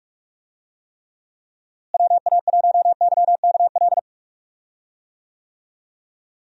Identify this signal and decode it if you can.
Morse code, heard as a keyed tone, about 41 wpm, 700 Hz. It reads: WA1YKL
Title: USA Call Signs - 6 Characters - Repeated 2x - 45wpm